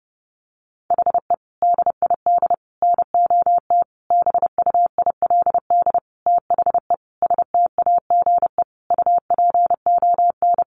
Morse code, heard as a keyed tone, about 30 wpm, 700 Hz. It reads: HE DID NOT BUILD THE STAGE UPON
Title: Sentences from Top 1200 Words 30wpm